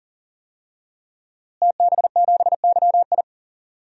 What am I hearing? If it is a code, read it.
Morse code, heard as a keyed tone, about 40 wpm, 700 Hz. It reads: TB7YI
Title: DXCC-100 Call Signs - 5 Characters - Repeated 2x - 40wpm